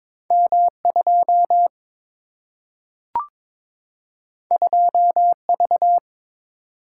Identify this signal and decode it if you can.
Morse code, heard as a keyed tone, about 22 wpm, 700 Hz. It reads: M2 2V
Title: Single-Letter Single-Number 22wpm